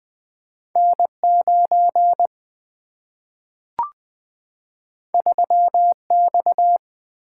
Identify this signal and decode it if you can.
Morse code, heard as a keyed tone, about 20 wpm, 700 Hz. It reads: N 9 3 X
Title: Two Letter-Number Combinations - 20wpm